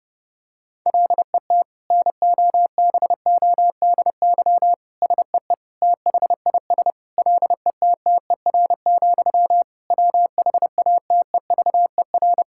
Morse code, heard as a keyed tone, about 32 wpm, 700 Hz. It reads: LET NOBODY SEE THIS LETTER, WHATEVER
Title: Sentences from Top 1600 Words 30wpm